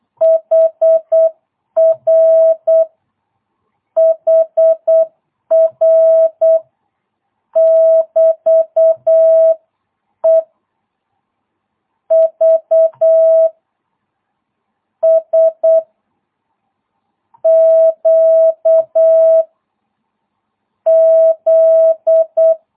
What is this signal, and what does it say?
Morse code, heard as a keyed tone, about 8 wpm, 660 Hz. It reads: HR HR <BT> E V S Q Z